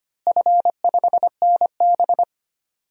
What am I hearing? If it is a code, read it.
Morse code, heard as a keyed tone, about 25 wpm, 700 Hz. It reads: F5NB
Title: DXCC-100 Call Signs - 4 Characters - Repeated 2x - 25wpm